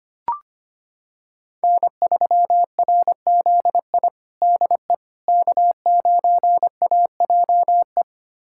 Morse code, heard as a keyed tone, about 25 wpm, 700 Hz. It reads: N3RZI DE K9AJE
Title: USA Callsign DE Callsign - 5 Characters - 25wpm